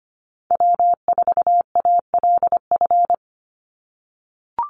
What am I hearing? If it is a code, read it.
Morse code, heard as a keyed tone, about 25 wpm, 700 Hz. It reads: W4ALF
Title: Call Signs - 25wpm